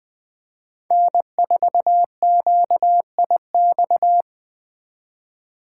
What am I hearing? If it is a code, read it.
Morse code, heard as a keyed tone, about 23 wpm, 700 Hz. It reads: N4QIX
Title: Speed Racing - USA 1x3 Call Signs 20wpm